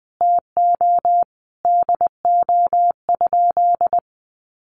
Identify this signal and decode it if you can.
Morse code, heard as a keyed tone, about 20 wpm, 700 Hz. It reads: TO DO?
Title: Sentences from Top 100 Words 20wpm